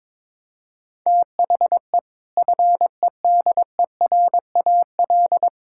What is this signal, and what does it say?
Morse code, heard as a keyed tone, about 22 wpm, 700 Hz. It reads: THE FEDERAL
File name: practice / 22wpm / Sentences with 3-Word Phrases - 22wpm